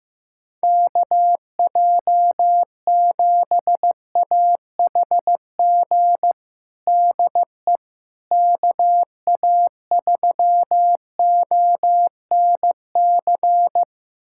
Morse code, heard as a keyed tone, about 15 wpm, 700 Hz. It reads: KJ7AHG DE KA3ONC
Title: USA Callsign DE Callsign 15wpm